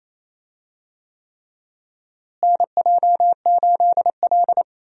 Morse code, heard as a keyed tone, about 28 wpm, 700 Hz. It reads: NJ8L
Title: USA Call Signs - 4 Characters - Repeated 3x - 28wpm